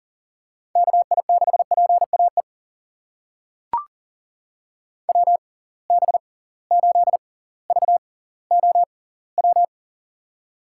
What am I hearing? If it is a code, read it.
Morse code, heard as a keyed tone, about 40 wpm, spaced as for 20 wpm, 700 Hz. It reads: KI6PAE WB8VOW
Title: USA Call Signs - 6 Characters - 6x Character Spacing - 40wpm